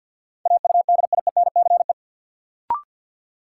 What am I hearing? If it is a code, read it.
Morse code, heard as a keyed tone, about 51 wpm, 700 Hz. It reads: AUDIENCE
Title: Sentences from Top 800 Words - New Word Drill - 50wpm